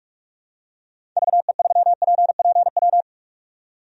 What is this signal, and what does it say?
Morse code, heard as a keyed tone, about 44 wpm, 700 Hz. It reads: VE3PPW
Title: DXCC-100 Call Signs - 6 Characters - 45wpm